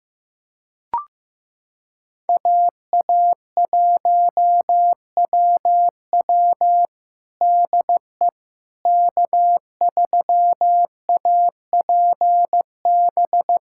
Morse code, heard as a keyed tone, about 15 wpm, 700 Hz. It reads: AA1WW DE K3APB
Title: USA Callsign DE Callsign - 5 Characters - 15wpm